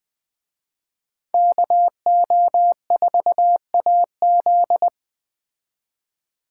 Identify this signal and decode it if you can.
Morse code, heard as a keyed tone, about 21 wpm, 700 Hz. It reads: KO4AZ
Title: Speed Racing - USA Call Signs 17wpm